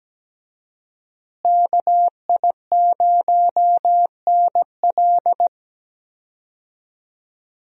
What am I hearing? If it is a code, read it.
Morse code, heard as a keyed tone, about 17 wpm, 700 Hz. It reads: KI0NL
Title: USA Call Signs - 5 Characters - Repeated 3x - 17wpm